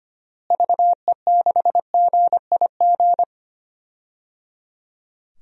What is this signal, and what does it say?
Morse code, heard as a keyed tone, about 25 wpm, 700 Hz. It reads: VE6GIG
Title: DXCC-100 Call Signs - 6 Characters - Repeated 3x - 25wpm